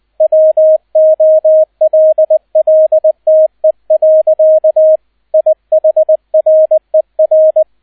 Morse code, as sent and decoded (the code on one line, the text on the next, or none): .-- --- .-.. .-.. - . .-.-.- / .. .... .-. .-.
WOLLTE. IHRER